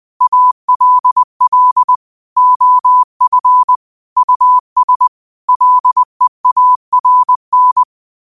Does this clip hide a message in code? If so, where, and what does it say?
Yes: Morse code from 0.2 to 7.8 s, heard as a keyed tone, about 20 wpm, 1000 Hz. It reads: ALL OF US LEARN